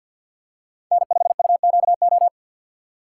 Morse code, heard as a keyed tone, about 52 wpm, 700 Hz. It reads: N5UQO
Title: USA Call Signs - 5 Characters - 4x Character Spacing - 50wpm